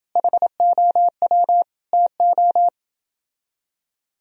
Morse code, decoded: HOW TO

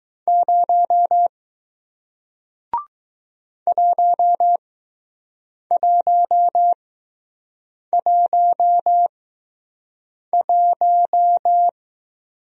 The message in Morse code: 0 1111

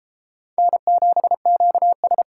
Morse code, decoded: N7QS

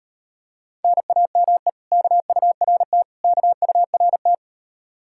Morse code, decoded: NAME KURT KURT

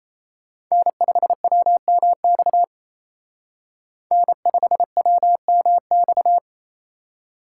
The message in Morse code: N5WMX N5WMX

